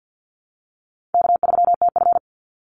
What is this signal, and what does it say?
Morse code, heard as a keyed tone, about 50 wpm, 700 Hz. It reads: K3TF
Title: Easy Call Signs Repeated 3x - 50wpm